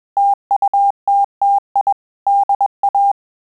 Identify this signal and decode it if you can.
Morse code, heard as a keyed tone, about 21 wpm, 800 Hz. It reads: TUTTI DA
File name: Telegraphy-group / breve testo midi